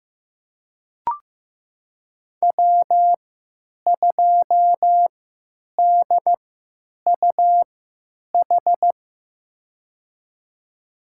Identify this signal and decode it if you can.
Morse code, heard as a keyed tone, about 15 wpm, spaced as for 9 wpm, 700 Hz. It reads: W2DUH
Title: USA Call Signs - 3x Character Spacing - 15wpm